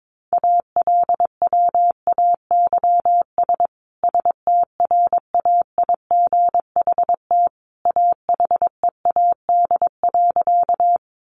Morse code, heard as a keyed tone, about 22 wpm, 700 Hz. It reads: ALWAYS STRAIGHT AHEAD.
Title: Sentences from Top 1100 Words 22wpm